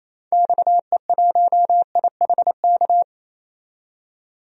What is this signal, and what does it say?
Morse code, heard as a keyed tone, about 28 wpm, 700 Hz. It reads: XE1IHK
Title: DXCC-100 Call Signs - 6 Characters - 28wpm